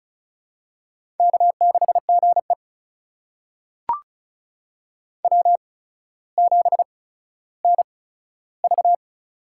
Morse code, heard as a keyed tone, about 34 wpm, spaced as for 15 wpm, 700 Hz. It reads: K6GE W7NV
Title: USA Call Signs - 4 Characters - 8x Character Spacing - 35wpm